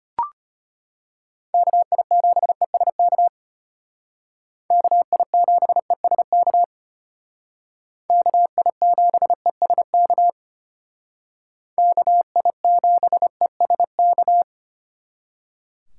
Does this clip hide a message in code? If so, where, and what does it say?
Yes: Morse code from 1.5 to 14.4 s, heard as a keyed tone, about 32 wpm, 700 Hz. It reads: KI7ESK KI7ESK KI7ESK KI7ESK